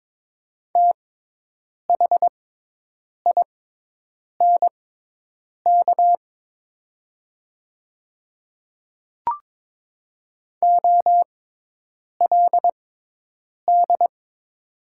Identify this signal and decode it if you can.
Morse code, heard as a keyed tone, about 22 wpm, spaced as for 8 wpm, 700 Hz. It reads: THINK OLD
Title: 6x Character Spacing - Top 200 Words - 22wpm